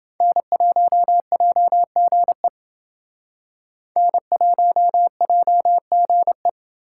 Morse code, heard as a keyed tone, about 28 wpm, 700 Hz. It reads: N1JGE N1JGE